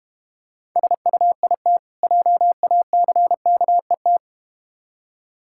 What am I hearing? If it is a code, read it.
Morse code, heard as a keyed tone, about 32 wpm, 700 Hz. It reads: SUIT JACKET